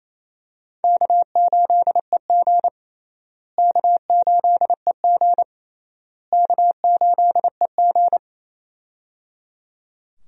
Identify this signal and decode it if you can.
Morse code, heard as a keyed tone, about 28 wpm, 700 Hz. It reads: K8EG K8EG K8EG
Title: USA Call Signs - 4 Characters - Repeated 3x - 28wpm